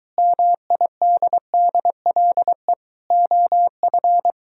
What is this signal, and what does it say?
Morse code, heard as a keyed tone, about 23 wpm, 700 Hz. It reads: MIDDLE OF